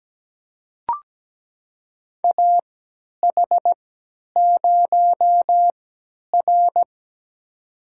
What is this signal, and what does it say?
Morse code, heard as a keyed tone, about 17 wpm, spaced as for 10 wpm, 700 Hz. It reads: AH0R
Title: USA Call Signs - 4 Characters - 3x Character Spacing - 17wpm